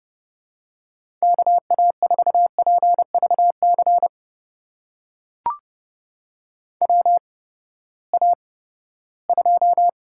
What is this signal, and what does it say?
Morse code, heard as a keyed tone, about 32 wpm, spaced as for 17 wpm, 700 Hz. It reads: KA4PVC WA2
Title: USA Call Signs - 6 Characters - 8x Character Spacing - 30wpm